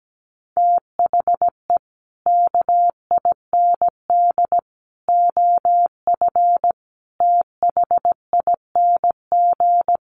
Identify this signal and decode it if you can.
Morse code, heard as a keyed tone, about 17 wpm, 700 Hz. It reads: THE KIND OF THING
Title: Sets of 5 Words Encore 17wpm